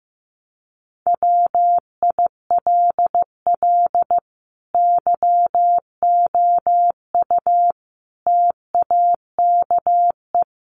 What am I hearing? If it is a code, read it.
Morse code, heard as a keyed tone, about 15 wpm, 700 Hz. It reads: WILL YOU TAKE
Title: Sentences from Top 900 Words 15wpm